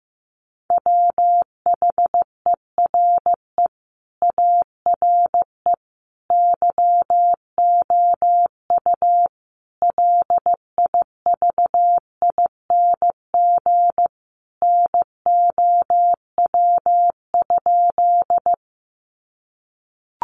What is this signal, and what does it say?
Morse code, heard as a keyed tone, about 15 wpm, 700 Hz. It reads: WHERE ARE YOU LIVING NOW?
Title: Sentences from Top 1100 Words 15wpm